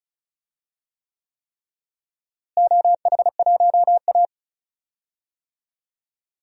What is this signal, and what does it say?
Morse code, heard as a keyed tone, about 36 wpm, 700 Hz. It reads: OH1A
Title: DXCC-100 Call Signs - 4 Characters - Repeated 3x - 35wpm